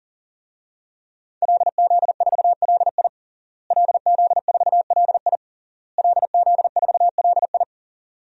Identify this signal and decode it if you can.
Morse code, heard as a keyed tone, about 37 wpm, 700 Hz. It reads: LZ4LI LZ4LI LZ4LI
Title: DXCC-100 Call Signs - 5 Characters - Repeated 3x - 40wpm